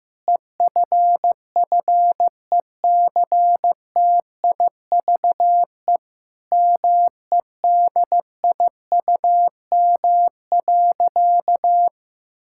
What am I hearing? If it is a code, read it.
Morse code, heard as a keyed tone, about 15 wpm, 700 Hz. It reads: EFFECTIVE MEDIUM.
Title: Sentences from Top 1000 Words - Review Part 6 - 15wpm